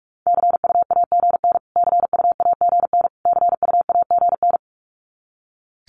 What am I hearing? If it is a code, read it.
Morse code, heard as a keyed tone, about 44 wpm, 700 Hz. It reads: CUAGN CUAGN CUAGN
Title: Abbreviations Repeated 3x - 45wpm